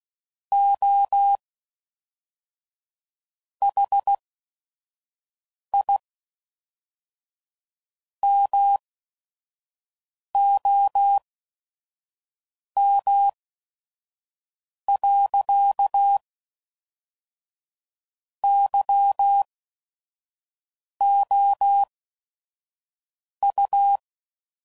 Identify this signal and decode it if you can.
Morse code, heard as a keyed tone, about 16 wpm, spaced as for 5 wpm, 800 Hz. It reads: OH I MOM. YOU
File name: CW Code Groups @5WPM / QSOs / QSO WA7QOX 1